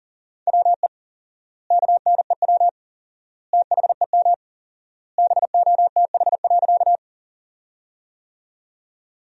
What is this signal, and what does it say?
Morse code, heard as a keyed tone, about 40 wpm, 700 Hz. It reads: WE KNEW THEM BOTH.
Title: 4x Word-Spacing - Sentences from Top 400 Words - 40wpm